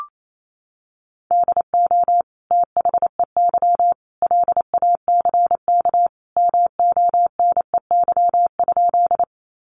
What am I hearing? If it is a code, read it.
Morse code, heard as a keyed tone, about 28 wpm, 700 Hz. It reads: DO THEY LACK MONEY?
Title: Sentences from Top 1100 Words 28wpm